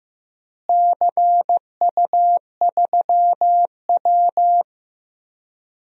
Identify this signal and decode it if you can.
Morse code, heard as a keyed tone, about 16 wpm, 700 Hz. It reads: CU3W
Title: DXCC-100 Call Signs - 4 Characters - Repeated 3x - 15wpm